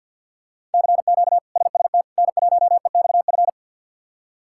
Morse code, heard as a keyed tone, about 46 wpm, 700 Hz. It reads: CQ SST N1EXF
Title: K1USN SST QSO Practice - 50wpm